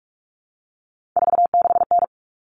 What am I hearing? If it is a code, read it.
Morse code, heard as a keyed tone, about 44 wpm, 700 Hz. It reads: 46N
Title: All Cut Numbers - 3-Digit 45wpm